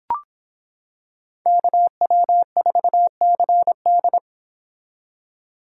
Morse code, heard as a keyed tone, about 26 wpm, 700 Hz. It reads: KW4CD